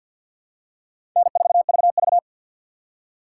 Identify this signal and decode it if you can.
Morse code, heard as a keyed tone, about 53 wpm, 700 Hz. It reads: N4VV
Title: USA Call Signs - 4x Character Spacing - 50wpm